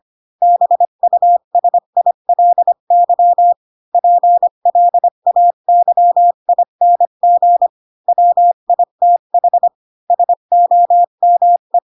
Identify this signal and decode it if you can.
Morse code, heard as a keyed tone, about 25 wpm, 700 Hz. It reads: BUSILY PLAYING WITH SOME
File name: Pinocchio / Pinocchio ch7 20 25 ik1hge